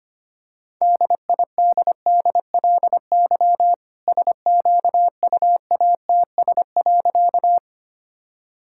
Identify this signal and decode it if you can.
Morse code, heard as a keyed tone, about 25 wpm, 700 Hz. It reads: DIDDLY SQUATS.
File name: practice / 25wpm / All-Age Appropriate Jokes - 25wpm